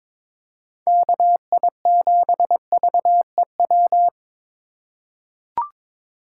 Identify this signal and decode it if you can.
Morse code, heard as a keyed tone, about 23 wpm, 700 Hz. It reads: KI7VEW
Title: USA Call Signs - 6 Characters - 3x Character Spacing - 22wpm